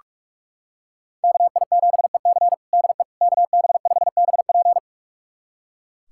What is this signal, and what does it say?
Morse code, heard as a keyed tone, about 45 wpm, 700 Hz. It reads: KI7EC DE KB5BP
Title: USA Callsign DE Callsign - 5 Characters - 45wpm